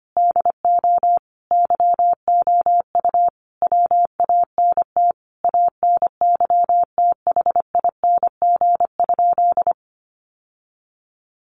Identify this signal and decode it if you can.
Morse code, heard as a keyed tone, about 25 wpm, 700 Hz. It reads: DO YOU WANT ANYTHING?
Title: Sentences from Top 1000 Words 25wpm